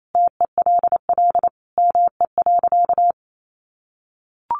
Morse code, heard as a keyed tone, about 28 wpm, 700 Hz. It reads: TELL ME.
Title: Sentences from Top 200 Words 28wpm